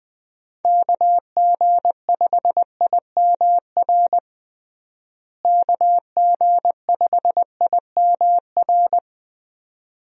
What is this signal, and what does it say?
Morse code, heard as a keyed tone, about 21 wpm, 700 Hz. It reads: KG5IMR KG5IMR
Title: USA Call Signs - 6 Characters - Repeated 2x - 20wpm